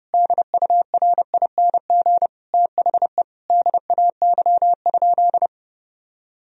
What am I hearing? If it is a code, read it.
Morse code, heard as a keyed tone, about 30 wpm, 700 Hz. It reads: DURING THE DAY?
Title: Sentences from Top 1000 Words - Review Part 2 - 30wpm